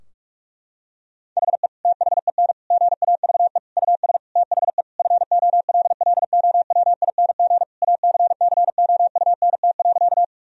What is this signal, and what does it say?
Morse code, heard as a keyed tone, about 47 wpm, 700 Hz. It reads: HE THEN GAVE US THE FOLLOWING ACCOUNT.
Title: Sentences from Top 1000 Words - Review Part 4 - 45wpm